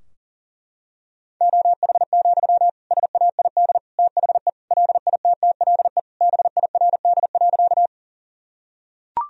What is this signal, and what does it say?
Morse code, heard as a keyed tone, about 40 wpm, 700 Hz. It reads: OH, SAID THE LITTLE BIRD.
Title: Sentences from Top 1000 Words - Review Part 2 - 40wpm